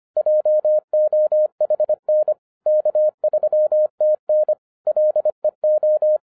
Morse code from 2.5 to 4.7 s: K3TN